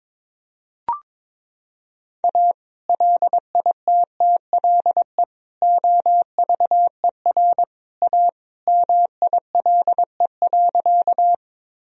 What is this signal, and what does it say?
Morse code, heard as a keyed tone, about 22 wpm, 700 Hz. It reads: A LITTLE OVER A MILE.
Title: Sentences from Top 1000 Words - Review Part 1 - 22wpm